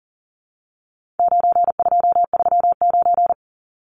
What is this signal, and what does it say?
Morse code, heard as a keyed tone, about 40 wpm, 700 Hz. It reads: 9239
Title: Numbers - 4-Digit 40wpm